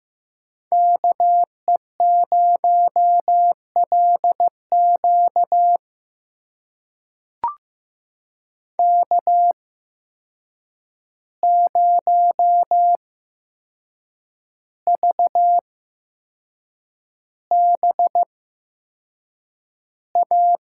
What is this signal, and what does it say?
Morse code, heard as a keyed tone, about 15 wpm, spaced as for 6 wpm, 700 Hz. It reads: KE0LQ K0VBA
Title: USA Call Signs - 5 Characters - 8x Character Spacing - 15wpm